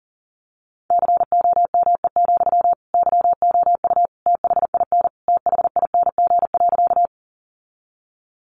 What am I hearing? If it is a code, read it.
Morse code, heard as a keyed tone, about 40 wpm, 700 Hz. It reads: COME, YOU THIN THING.
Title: Sentences from Top 1200 Words 40wpm